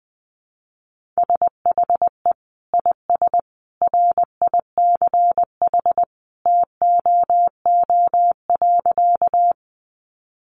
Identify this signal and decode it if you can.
Morse code, heard as a keyed tone, about 20 wpm, 700 Hz. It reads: SHE IS RICH TOO.
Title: Sentences from Top 1100 Words 20wpm